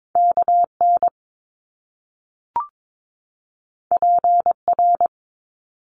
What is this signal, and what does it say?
Morse code, heard as a keyed tone, about 22 wpm, 700 Hz. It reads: KN PR
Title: Top Two Letter Combinations in Words 22wpm